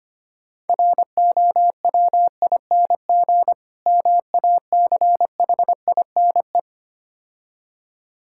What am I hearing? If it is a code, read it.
Morse code, heard as a keyed tone, about 26 wpm, 700 Hz. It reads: ROWING MACHINE
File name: practice / 25wpm / Difficult Words - Speed-Racing with Hints - 25wpm